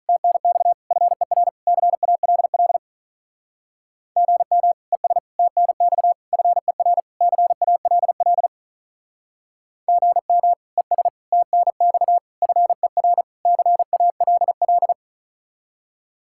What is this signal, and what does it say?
Morse code, heard as a keyed tone, about 45 wpm, 700 Hz. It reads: TNX FER CALL GM ES TNX FER CALL GM ES TNX FER CALL